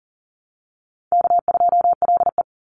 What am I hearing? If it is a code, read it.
Morse code, heard as a keyed tone, about 40 wpm, 700 Hz. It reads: K2LE
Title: Easy Call Signs - 40wpm